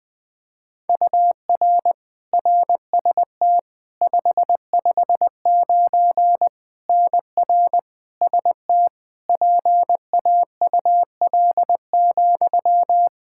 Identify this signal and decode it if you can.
Morse code, heard as a keyed tone, about 20 wpm, 700 Hz. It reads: UR RST 559 NR ST PAUL,
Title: Practice QSO - Exchange 1 - Beginning - 20wpm